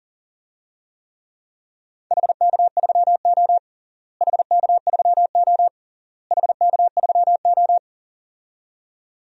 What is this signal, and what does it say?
Morse code, heard as a keyed tone, about 40 wpm, 700 Hz. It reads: HK3O HK3O HK3O
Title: DXCC-100 Call Signs - 4 Characters - Repeated 3x - 40wpm